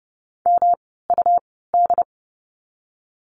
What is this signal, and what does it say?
Morse code, heard as a keyed tone, about 30 wpm, spaced as for 17 wpm, 700 Hz. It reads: MUD
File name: course / 30:10wpm / 108 Lesson 18f - Letter m - Words 30wpm